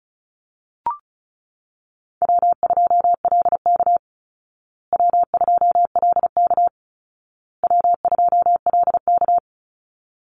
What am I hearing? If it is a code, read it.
Morse code, heard as a keyed tone, about 34 wpm, 700 Hz. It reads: W2LK W2LK W2LK
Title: Call Signs Repeated 2x - 35wpm